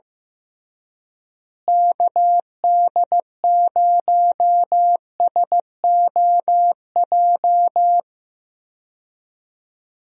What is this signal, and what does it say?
Morse code, heard as a keyed tone, about 15 wpm, 700 Hz. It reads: KD0SOJ
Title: USA Call Signs - 6 Characters - Repeated 2x - 15wpm